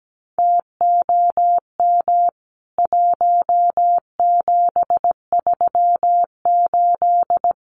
Morse code, heard as a keyed tone, about 17 wpm, 700 Hz. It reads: TOM 1738
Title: CWT Exchange 17wpm